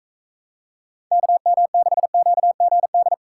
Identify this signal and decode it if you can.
Morse code, heard as a keyed tone, about 39 wpm, 700 Hz. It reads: KM6QGD